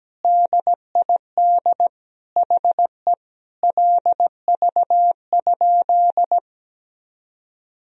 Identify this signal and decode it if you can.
Morse code, heard as a keyed tone, about 17 wpm, 700 Hz. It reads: DID HE LV?
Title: Sentences from Top 400 Words and Abbreviations 17wpm